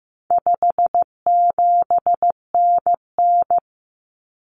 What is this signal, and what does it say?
Morse code, heard as a keyed tone, about 15 wpm, 700 Hz. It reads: 57NN
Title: Common Cut Numbers - 4-Digit 15wpm